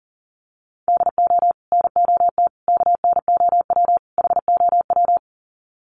Morse code, heard as a keyed tone, about 40 wpm, 700 Hz. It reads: DO NOT KNOW HOW
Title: Sets of 4 Words 40wpm